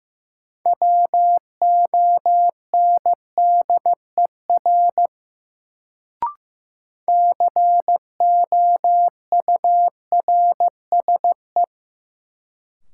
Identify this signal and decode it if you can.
Morse code, heard as a keyed tone, about 15 wpm, 700 Hz. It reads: WONDER COURSE
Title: Sentences from Top 500 Words - New Word Drill - 15wpm